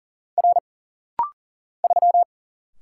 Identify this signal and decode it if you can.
Morse code, heard as a keyed tone, about 40 wpm, 700 Hz. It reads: R 3
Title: Single Letter-Number Alt - Rapid-Fire 40wpm